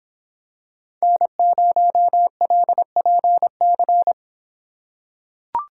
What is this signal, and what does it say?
Morse code, heard as a keyed tone, about 26 wpm, 700 Hz. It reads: N0LPC